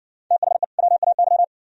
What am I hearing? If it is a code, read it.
Morse code, heard as a keyed tone, about 60 wpm, 700 Hz. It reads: THE WAY